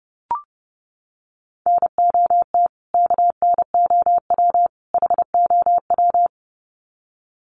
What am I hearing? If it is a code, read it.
Morse code, heard as a keyed tone, about 30 wpm, 700 Hz. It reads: NOT KNOW HOW
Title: Sets of 3 Words 30wpm